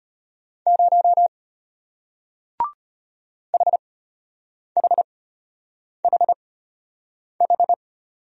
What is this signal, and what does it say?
Morse code, heard as a keyed tone, about 34 wpm, spaced as for 8 wpm, 700 Hz. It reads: 0 HHHH